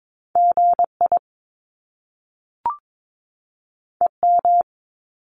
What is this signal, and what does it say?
Morse code, heard as a keyed tone, about 22 wpm, 700 Hz. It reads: GI EM
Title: Top Two Letter Combinations in Words 22wpm